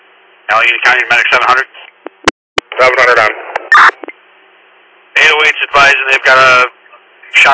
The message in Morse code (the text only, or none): none